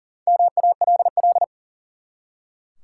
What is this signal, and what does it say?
Morse code, heard as a keyed tone, about 41 wpm, 700 Hz. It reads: MALL
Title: Four Letter Words - 40wpm